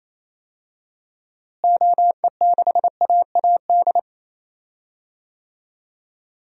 Morse code, heard as a keyed tone, about 28 wpm, 700 Hz. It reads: OE6AAD
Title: DXCC-100 Call Signs - 6 Characters - Repeated 3x - 28wpm